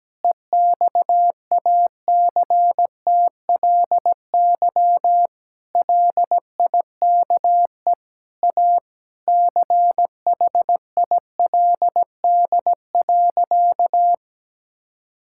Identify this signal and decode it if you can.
Morse code, heard as a keyed tone, about 17 wpm, 700 Hz. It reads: EXACTLY LIKE A CHILD.
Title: Sentences from Top 1000 Words - Review Part 6 - 17wpm